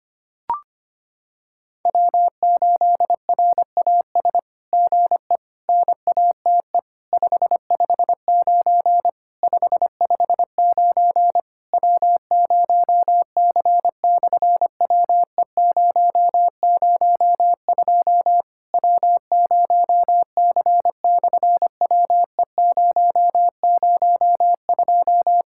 Morse code, heard as a keyed tone, about 25 wpm, 700 Hz. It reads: W8RAS GE NATE 559 559 W0C/WE002 W0C/WE002
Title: SOTA QSO Practice 25wpm